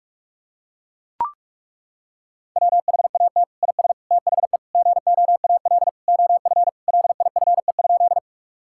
Morse code, heard as a keyed tone, about 41 wpm, 690 Hz. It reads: WHAT IS THE GOAL OF LIFE?
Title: Sentences from Top 1000 Words - Review Part 5 - 45wpm